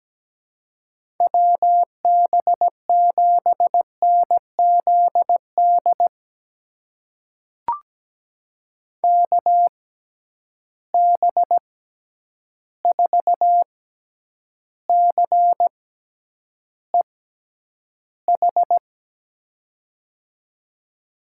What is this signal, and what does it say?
Morse code, heard as a keyed tone, about 17 wpm, spaced as for 9 wpm, 700 Hz. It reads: WB7NZD KB4CEH
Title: USA Call Signs - 6 Characters - 6x Character Spacing - 17wpm